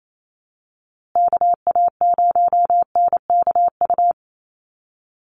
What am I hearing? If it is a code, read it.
Morse code, heard as a keyed tone, about 28 wpm, 700 Hz. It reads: KA0NKU
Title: USA Call Signs Repeated 3x 28wpm